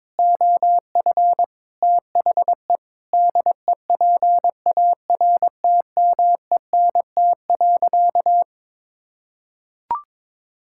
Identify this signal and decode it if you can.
Morse code, heard as a keyed tone, about 22 wpm, 700 Hz. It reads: OF THE DEPARTMENT.
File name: practice / 22wpm / Sentences with 4-Word Phrases - 22wpm